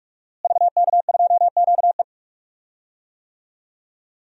Morse code, heard as a keyed tone, about 41 wpm, 700 Hz. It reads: VK2QE
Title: DXCC-100 Call Signs - 5 Characters - Repeated 3x - 45wpm